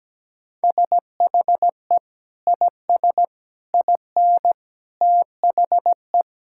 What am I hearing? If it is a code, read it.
Morse code, heard as a keyed tone, about 17 wpm, 700 Hz. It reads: SHE IS IN THE
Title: Sentences from Top 1000 Words - Review Part 5 - 17wpm